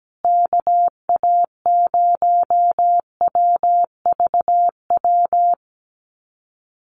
Morse code, heard as a keyed tone, about 17 wpm, 700 Hz. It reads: KA0WVW
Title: USA Call Signs Repeated 3x 17wpm